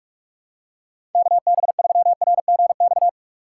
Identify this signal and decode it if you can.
Morse code, heard as a keyed tone, about 47 wpm, 700 Hz. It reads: KB3RGX
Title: USA Call Signs - 6 Characters - 3x Character Spacing - 45wpm